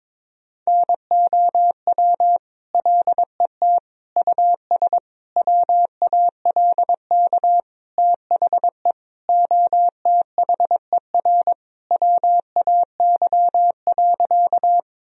Morse code, heard as a keyed tone, about 23 wpm, 700 Hz. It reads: NOW LET US WALK THE OTHER WAY.
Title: Sentences from Top 1000 Words - Review Part 1 - 22wpm